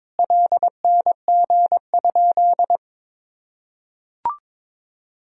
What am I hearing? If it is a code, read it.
Morse code, heard as a keyed tone, about 22 wpm, 700 Hz. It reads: LNG?
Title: Sentences from Top 500 Words and Abbreviations 22wpm